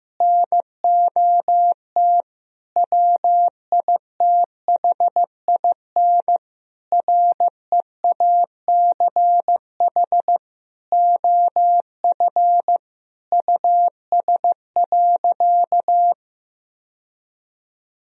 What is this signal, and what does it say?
Morse code, heard as a keyed tone, about 15 wpm, 700 Hz. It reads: NOT WITHIN REACH OF US.
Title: Sentences from Top 1000 Words - Review Part 4 - 15wpm